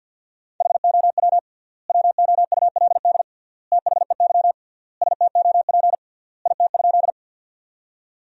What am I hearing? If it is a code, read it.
Morse code, heard as a keyed tone, about 50 wpm, 700 Hz. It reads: HOW WOULD THEY STOP IT?
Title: 3x Word-Spacing - Sentences from Top 300 Words - 50wpm